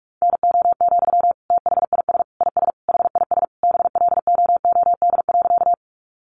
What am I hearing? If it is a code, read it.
Morse code, heard as a keyed tone, about 44 wpm, 700 Hz. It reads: NO, THIS IS HIS BLOOD.
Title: Sentences from Top 1100 Words 45wpm